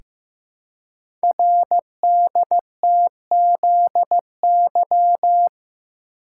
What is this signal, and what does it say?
Morse code, heard as a keyed tone, about 15 wpm, 690 Hz. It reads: RDTZY
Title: Code Groups - 5 Letters - 15wpm